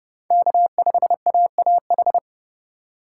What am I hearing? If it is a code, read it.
Morse code, heard as a keyed tone, about 28 wpm, 700 Hz. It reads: K5AAH